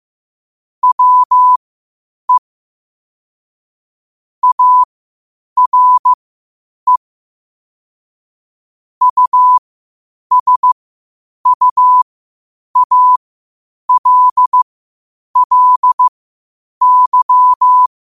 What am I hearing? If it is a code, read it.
Morse code, heard as a keyed tone, about 15 wpm, spaced as for 8 wpm, 1000 Hz. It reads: WE ARE USUALLY
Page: New quotes every day in morse code at 8 Words per minute.